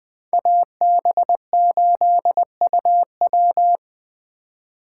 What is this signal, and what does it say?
Morse code, heard as a keyed tone, about 24 wpm, 700 Hz. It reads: AB8UW
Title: Speed Racing - USA 2x2 Call Signs 20wpm